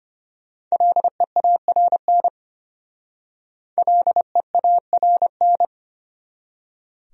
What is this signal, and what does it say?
Morse code, heard as a keyed tone, about 33 wpm, 700 Hz. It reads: LEARN LEARN